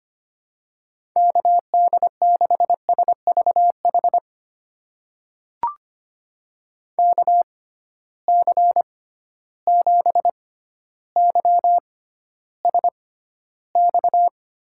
Morse code, heard as a keyed tone, about 25 wpm, spaced as for 13 wpm, 700 Hz. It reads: KD6SVH KC7YSX
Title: USA Call Signs - 6 Characters - 6x Character Spacing - 25wpm